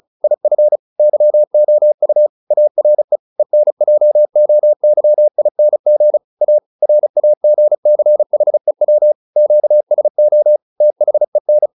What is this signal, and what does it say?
Morse code, heard as a keyed tone, about 35 wpm, 600 Hz. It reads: IF YOU ARE ENJOYING A RAGCHEW QSO THEN